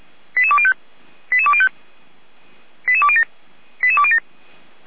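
Five-tone selective call (ZVEI), heard as quick runs of five tones, tones of about 70 ms. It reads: 88205 88205 88207 88207
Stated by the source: ZVEI five-tone selective call 88179 received by SDR.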